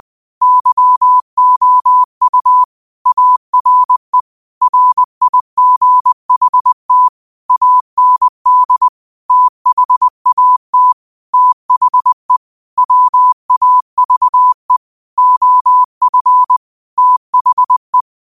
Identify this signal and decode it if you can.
Morse code, heard as a keyed tone, about 20 wpm, 1000 Hz. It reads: YOU ARE RIGHT AND THAT THE WAVE OF THE